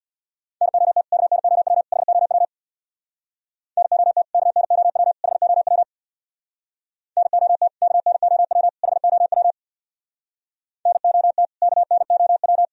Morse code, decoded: NOT KNOW HOW NOT KNOW HOW NOT KNOW HOW NOT KNOW

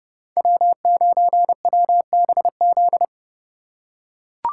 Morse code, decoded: W9WBZ

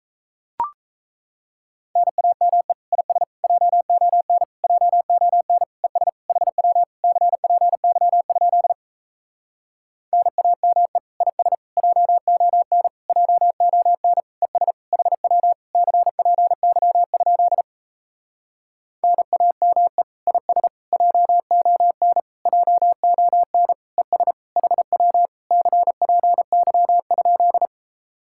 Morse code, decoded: NAME IS JON JON ES HW CPY? NAME IS JON JON ES HW CPY? NAME IS JON JON ES HW CPY?

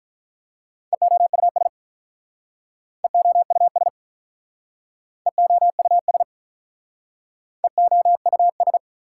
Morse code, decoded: EOUS EOUS EOUS EOUS

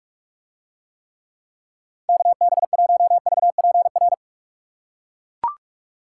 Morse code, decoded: KB1VPR

